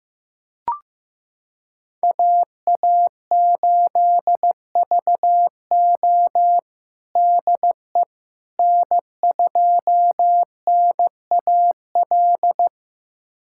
Morse code, heard as a keyed tone, about 15 wpm, 700 Hz. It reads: AA8VO DE N2NAL
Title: USA Callsign DE Callsign - 5 Characters - 15wpm